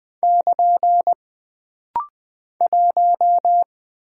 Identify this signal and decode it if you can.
Morse code, heard as a keyed tone, about 20 wpm, 700 Hz. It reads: <KN> 1
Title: Single Character (Letter:Number:Punctuation:Prosign) - Rapid-Fire - 20wpm